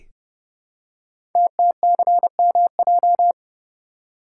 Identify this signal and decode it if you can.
Morse code, heard as a keyed tone, about 30 wpm, 700 Hz. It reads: TTCMJ
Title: Code Groups - 5 Letters - 30wpm